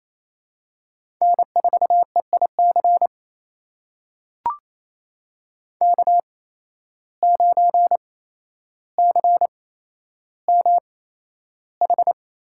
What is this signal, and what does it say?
Morse code, heard as a keyed tone, about 28 wpm, spaced as for 12 wpm, 700 Hz. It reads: N4EIC K9CMH
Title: USA Call Signs - 8x Character Spacing - 28wpm